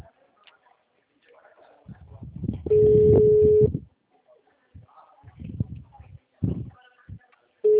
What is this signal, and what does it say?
Ringback tone (Europe, 425 Hz).
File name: out- -2003- - - .248